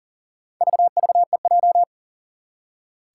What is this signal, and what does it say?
Morse code, heard as a keyed tone, about 40 wpm, 700 Hz. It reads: VVEJ
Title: Four Challenging Letter-Number Combinations 40wpm